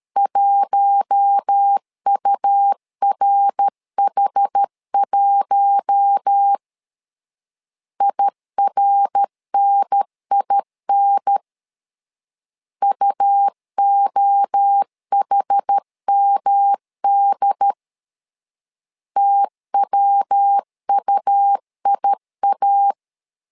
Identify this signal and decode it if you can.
Morse code, heard as a keyed tone, about 13 wpm, 800 Hz. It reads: 1URH1 IRNIN UOHMD TWUIA
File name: LEZIONI / 4 Q1 12wpm 2028